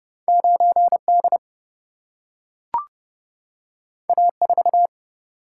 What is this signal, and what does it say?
Morse code, heard as a keyed tone, about 31 wpm, 700 Hz. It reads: 9 D A 4
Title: Single-Letter Single-Number 30wpm